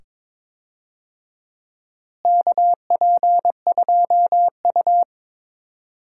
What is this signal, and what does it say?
Morse code, heard as a keyed tone, about 21 wpm, 700 Hz. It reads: KP2U
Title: DXCC-100 Call Signs - 4 Characters - Repeated 2x - 22wpm